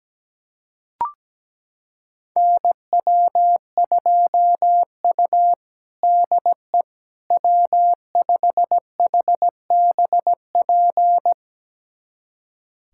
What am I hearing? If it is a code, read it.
Morse code, heard as a keyed tone, about 17 wpm, 700 Hz. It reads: NW2U DE W5HBP
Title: USA Callsign DE Callsign 17wpm